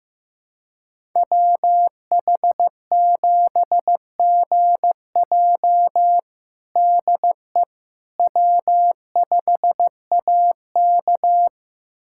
Morse code, heard as a keyed tone, about 15 wpm, 700 Hz. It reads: WH7GJ DE W5AK
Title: USA Callsign DE Callsign 15wpm